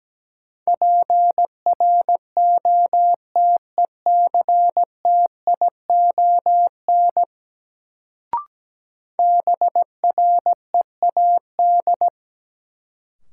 Morse code, heard as a keyed tone, about 17 wpm, 700 Hz. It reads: PROTECTION BREAD
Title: Sentences from Top 1600 Words - New Word Drill - 17wpm